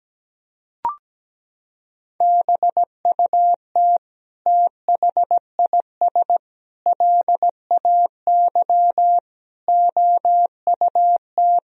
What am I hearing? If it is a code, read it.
Morse code, heard as a keyed tone, about 17 wpm, 700 Hz. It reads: BUT THIS LAY OUT
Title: Sentences from Top 1000 Words - Review Part 2 - 17wpm